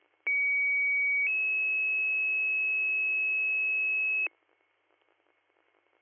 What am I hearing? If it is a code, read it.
Two-tone page, heard as a one-second tone, then a three-second tone, about 2250 then 2600 Hz, with noise. It is heard from 0.3 to 4.3 s.